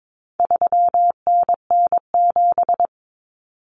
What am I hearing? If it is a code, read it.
Morse code, heard as a keyed tone, about 22 wpm, 700 Hz. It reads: 3NN7
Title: Common Cut Numbers - 4-Digit 22wpm